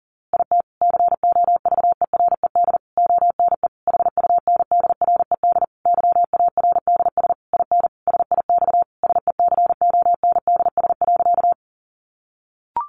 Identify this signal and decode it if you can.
Morse code, heard as a keyed tone, about 40 wpm, 700 Hz. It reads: IT COVERED ONE HUNDRED YARDS IN SIX SECONDS.
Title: A Sound Of Thunder 40wpm